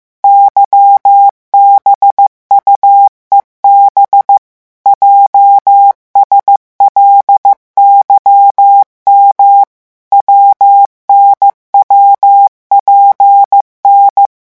Morse code, heard as a keyed tone, about 15 wpm, 800 Hz. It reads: YBUEB JSLYM WNWPN
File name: morse mp3 / revision8-vitesse 14 mots